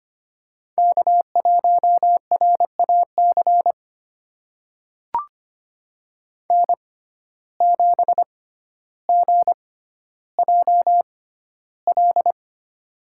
Morse code, heard as a keyed tone, about 25 wpm, spaced as for 13 wpm, 700 Hz. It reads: K1RAC N7GJL
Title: USA Call Signs - 5 Characters - 6x Character Spacing - 25wpm